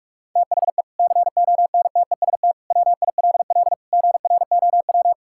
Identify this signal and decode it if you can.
Morse code, heard as a keyed tone, about 41 wpm, 700 Hz. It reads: THE CONTEST WILL GROW
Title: Sentences from Top 1600 Words 45wpm